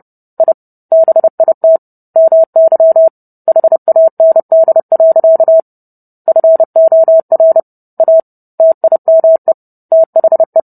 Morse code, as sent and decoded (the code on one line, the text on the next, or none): .. / -... .. - / -- -.-- / .... .- -. -.. .-.-.- / ..-. --- .-. / .- / - .. -- . / - .... .
I BIT MY HAND. FOR A TIME THE